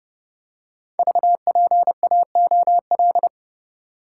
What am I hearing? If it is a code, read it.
Morse code, heard as a keyed tone, about 30 wpm, 700 Hz. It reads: VPAOL
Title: Code Groups - 5 Letters - 30wpm